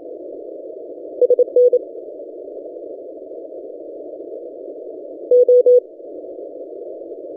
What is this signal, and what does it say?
Morse code, heard as a keyed tone, about 28 wpm, 490 Hz. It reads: SN O
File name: ft5zm-17cw-2014-01-27-0120z-lp